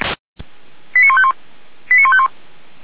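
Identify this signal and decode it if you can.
Five-tone selective call (ZVEI), heard as quick runs of five tones, tones of about 70 ms. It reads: 89151 89151
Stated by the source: ZVEI five-tone selective call received by SDR.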